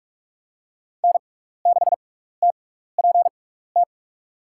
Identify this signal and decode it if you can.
Morse code, heard as a keyed tone, about 46 wpm, spaced as for 15 wpm, 700 Hz. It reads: N6TPT
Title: USA Call Signs - 6x Character Spacing - 45wpm